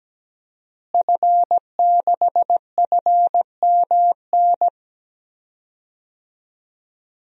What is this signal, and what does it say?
Morse code, heard as a keyed tone, about 17 wpm, 700 Hz. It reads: F6FMN
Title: DXCC-100 Call Signs - 5 Characters - Repeated 3x - 17wpm